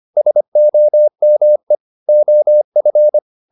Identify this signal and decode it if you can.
Morse code, heard as a keyed tone, about 25 wpm, 600 Hz. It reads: SOME OF